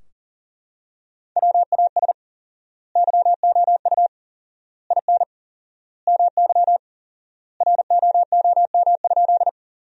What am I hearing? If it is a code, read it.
Morse code, heard as a keyed tone, about 40 wpm, 700 Hz. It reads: WAS YOU IN MY ROOM?
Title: 4x Word-Spacing - Sentences from Top 300 Words - 40wpm